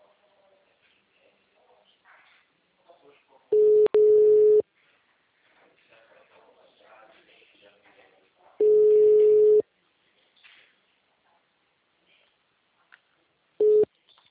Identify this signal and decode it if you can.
Ringback tone (Europe, 425 Hz), with speech.